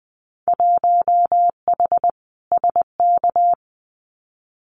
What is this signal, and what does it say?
Morse code, heard as a keyed tone, about 20 wpm, 700 Hz. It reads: 1H SK
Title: Winter Field Day Exchanges 20wpm